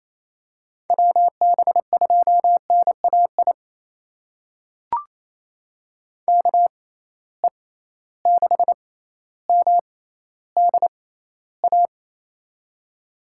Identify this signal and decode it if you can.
Morse code, heard as a keyed tone, about 28 wpm, spaced as for 14 wpm, 700 Hz. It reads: WB2NAI KE6MDA
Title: USA Call Signs - 6 Characters - 6x Character Spacing - 28wpm